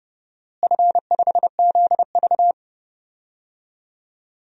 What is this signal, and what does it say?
Morse code, heard as a keyed tone, about 30 wpm, 700 Hz. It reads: F5ZV
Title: DXCC-100 Call Signs - 4 Characters - Repeated 2x - 30wpm